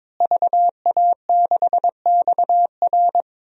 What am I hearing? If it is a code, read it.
Morse code, heard as a keyed tone, about 22 wpm, 700 Hz. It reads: VA6XR
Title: DXCC-100 Call Signs - 5 Characters - Repeated 3x - 22wpm